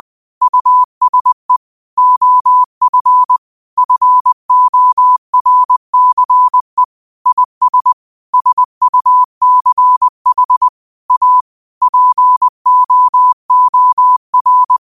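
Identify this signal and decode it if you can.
Morse code, heard as a keyed tone, about 20 wpm, 1000 Hz. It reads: USE OF FORCE IS SUCH A POOR
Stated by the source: Morse code quote of the day at 20 wpm.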